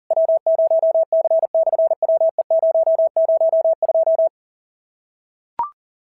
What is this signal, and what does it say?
Morse code, heard as a keyed tone, about 37 wpm, 650 Hz. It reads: W0C/WE002
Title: SOTA QSO Practice 40wpm